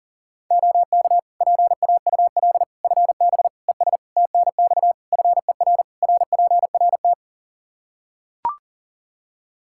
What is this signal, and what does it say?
Morse code, heard as a keyed tone, about 40 wpm, 700 Hz. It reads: OK PAUL FB ES TNX FER RPRT
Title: Practice QSO - Exchange 2 - Info - 40wpm